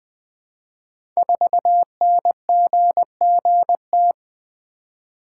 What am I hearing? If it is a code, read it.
Morse code, heard as a keyed tone, about 20 wpm, 700 Hz. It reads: 4NGGT
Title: Code Groups - 5 Letter:Numbers - 20wpm